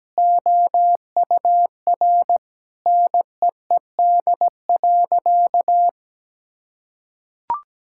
Morse code, heard as a keyed tone, about 17 wpm, 700 Hz. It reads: OUR NEED.
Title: Sentences from Top 1000 Words - Review Part 6 - 17wpm